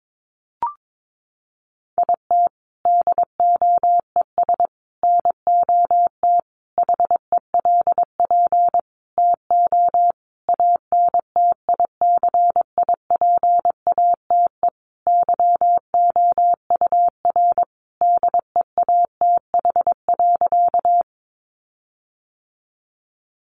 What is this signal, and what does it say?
Morse code, heard as a keyed tone, about 22 wpm, 700 Hz. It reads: IT DOES NOT HELP TO ANTICIPATE YOUR DEATH.
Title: Sentences from Top 1500 Words 22wpm